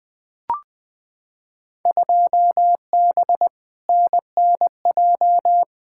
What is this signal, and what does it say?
Morse code, heard as a keyed tone, about 20 wpm, 700 Hz. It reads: 2B NNJ
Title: Field Day Exchanges Encore - 20wpm